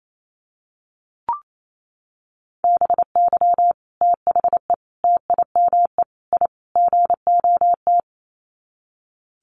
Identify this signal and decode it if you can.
Morse code, heard as a keyed tone, about 28 wpm, 700 Hz. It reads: BY THE TIME I GOT
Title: Sets of 5 Words Encore2 28wpm